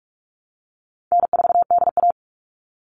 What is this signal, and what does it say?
Morse code, heard as a keyed tone, about 44 wpm, 700 Hz. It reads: N4DA
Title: NA SOTA Chaser Callsigns 45wpm